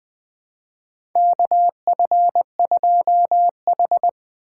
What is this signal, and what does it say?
Morse code, heard as a keyed tone, about 20 wpm, 700 Hz. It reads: K F 2 H
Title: USA Call Signs - 4 Characters - 6x Character Spacing - 20wpm